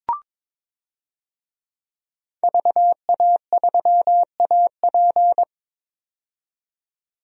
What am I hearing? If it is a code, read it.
Morse code, heard as a keyed tone, about 22 wpm, 700 Hz. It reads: VA3AP
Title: DXCC-100 Call Signs - 5 Characters - 22wpm